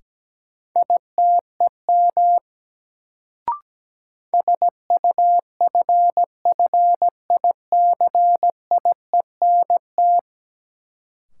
Morse code, heard as a keyed tone, about 17 wpm, 700 Hz. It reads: ITEM SUFFICIENT
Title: Sentences from Top 1400 Words - New Word Drill Alt - 17wpm